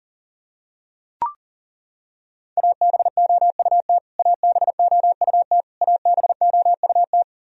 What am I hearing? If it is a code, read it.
Morse code, heard as a keyed tone, about 40 wpm, 700 Hz. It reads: ABOUT ABOUT ABOUT
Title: Top 100 Words Repeated 40wpm